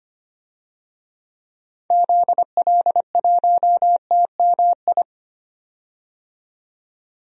Morse code, heard as a keyed tone, about 25 wpm, 700 Hz. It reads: ZL1TMI
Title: DXCC-100 Call Signs - 6 Characters - 25wpm